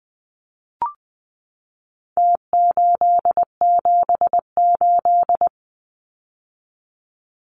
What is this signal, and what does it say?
Morse code, heard as a keyed tone, about 20 wpm, 700 Hz. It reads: T878
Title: Numbers - Contest Style - 4-Digit 20wpm